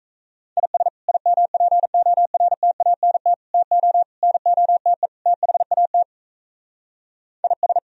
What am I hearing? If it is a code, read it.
Morse code, heard as a keyed tone, about 36 wpm, 700 Hz. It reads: IS IMPORTANT TO NOTE THAT IS